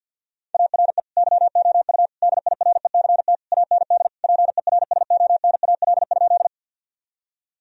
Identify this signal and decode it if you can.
Morse code, heard as a keyed tone, about 52 wpm, 700 Hz. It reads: ARE YOU DIRECT AND PERSONAL?
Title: Sentences from Top 1000 Words - Review Part 4 - 50wpm